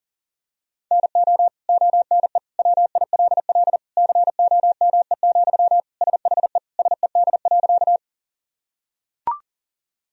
Morse code, heard as a keyed tone, about 40 wpm, 700 Hz. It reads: NO ONE WILL COME, SHE SED.
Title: Sentences from Top 100 Words and Abbreviations 40wpm